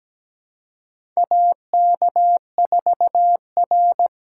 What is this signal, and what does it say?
Morse code, heard as a keyed tone, about 17 wpm, 700 Hz. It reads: AK4R
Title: USA Call Signs - 4 Characters - 2x Character Spacing - 17wpm